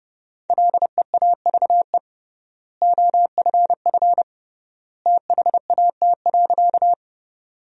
Morse code, heard as a keyed tone, about 30 wpm, 700 Hz. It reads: LEAVE OFF THAT.
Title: 3x Word-Spacing - Sentences from Top 400 Words - 30wpm